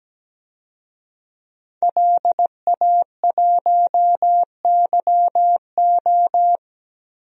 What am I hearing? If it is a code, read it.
Morse code, heard as a keyed tone, about 17 wpm, 700 Hz. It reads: LA1YO
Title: DXCC-100 Call Signs - 5 Characters - Repeated 3x - 17wpm